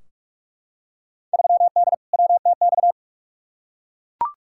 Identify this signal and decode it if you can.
Morse code, heard as a keyed tone, about 42 wpm, 700 Hz. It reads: 3D WTX
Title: Field Day Exchanges Encore - 45wpm